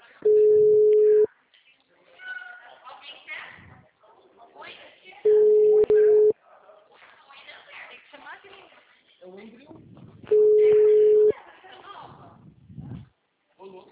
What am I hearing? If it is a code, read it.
Ringback tone (Europe, 425 Hz), with speech.